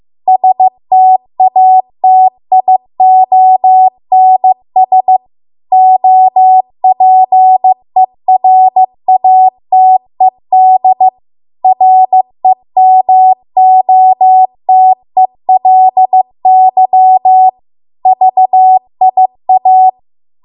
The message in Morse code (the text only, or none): STATIONS OPERATED REMOTELY VIA